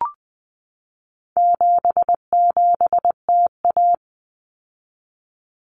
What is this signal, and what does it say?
Morse code, heard as a keyed tone, about 20 wpm, 700 Hz. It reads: 77TA
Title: Common Cut Numbers - 4-Digit 20wpm